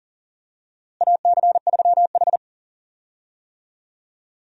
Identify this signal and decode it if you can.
Morse code, heard as a keyed tone, about 38 wpm, 700 Hz. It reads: AC3H
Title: USA Call Signs - 4 Characters - Repeated 2x - 40wpm